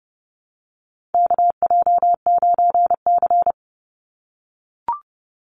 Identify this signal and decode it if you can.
Morse code, heard as a keyed tone, about 30 wpm, 700 Hz. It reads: KJ9C
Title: Call Signs - 30wpm